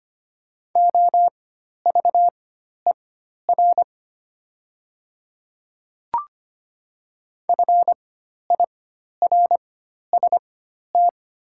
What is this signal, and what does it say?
Morse code, heard as a keyed tone, about 25 wpm, spaced as for 12 wpm, 700 Hz. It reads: OVER FIRST
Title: 4x Character Spacing - Top 100 Words - 25wpm